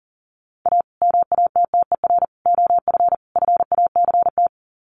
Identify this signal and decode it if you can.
Morse code, heard as a keyed tone, about 40 wpm, 700 Hz. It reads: A MATTER OF FACT
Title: Sets of 4 Words 40wpm